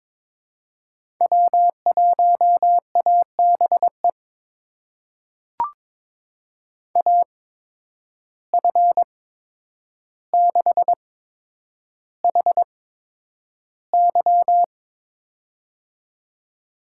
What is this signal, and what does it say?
Morse code, heard as a keyed tone, about 22 wpm, spaced as for 9 wpm, 700 Hz. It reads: W1ABE AF6HY
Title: USA Call Signs - 5 Characters - 8x Character Spacing - 22wpm